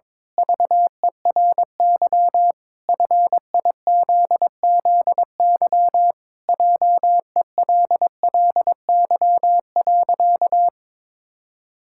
Morse code, heard as a keyed tone, about 22 wpm, 700 Hz. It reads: VERY FIZZY JELLY.
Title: Pangram Challenge v2 - 22wpm